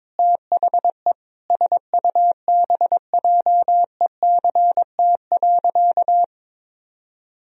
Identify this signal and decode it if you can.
Morse code, heard as a keyed tone, about 22 wpm, 700 Hz. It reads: THE SUBJECT.
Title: Sentences from Top 1000 Words - Review Part 3 - 22wpm